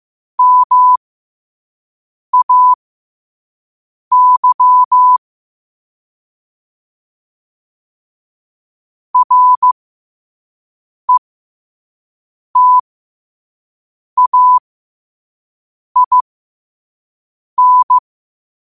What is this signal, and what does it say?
Morse code, heard as a keyed tone, about 15 wpm, spaced as for 5 wpm, 1000 Hz. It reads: MAY RETAIN
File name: qotdfiles / 5wpm